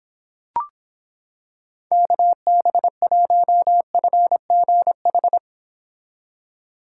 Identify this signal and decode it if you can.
Morse code, heard as a keyed tone, about 26 wpm, 700 Hz. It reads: KB1FGH